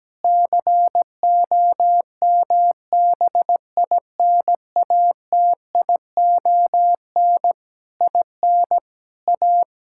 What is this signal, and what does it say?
Morse code, heard as a keyed tone, about 17 wpm, 700 Hz. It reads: COMBINATION IN A
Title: Sentences from Top 1000 Words - Review Part 6 - 17wpm